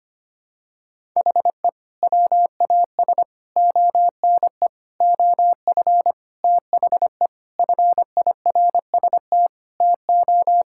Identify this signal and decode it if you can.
Morse code, heard as a keyed tone, about 25 wpm, 700 Hz. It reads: HE WAS ONE OF THE FIRST TO
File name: practice / 25wpm / Sentences with 3-Word Phrases - 25wpm